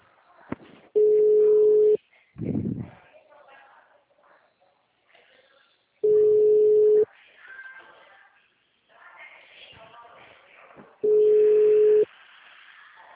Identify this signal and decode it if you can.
Ringback tone (Europe, 425 Hz), with speech.
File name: out- -2031- - - .66